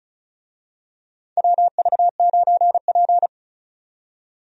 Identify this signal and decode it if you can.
Morse code, heard as a keyed tone, about 34 wpm, 700 Hz. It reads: WV9P
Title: USA Call Signs - 4 Characters - 35wpm